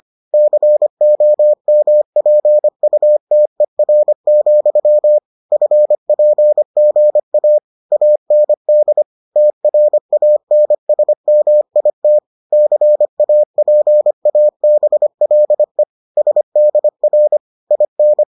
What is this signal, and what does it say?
Morse code, heard as a keyed tone, about 25 wpm, 600 Hz. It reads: COMPUTER, FPGA AND TRANSMIT CAPABLE SDR IN